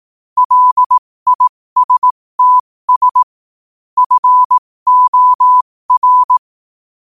Morse code, heard as a keyed tone, about 18 wpm, spaced as for 15 wpm, 1000 Hz. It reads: LISTS FOR
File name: qotdfiles / 15wpm